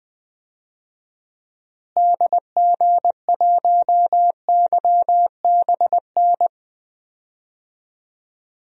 Morse code, heard as a keyed tone, about 20 wpm, 700 Hz. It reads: DG1YBN
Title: DXCC-100 Call Signs - 6 Characters - Repeated 2x - 20wpm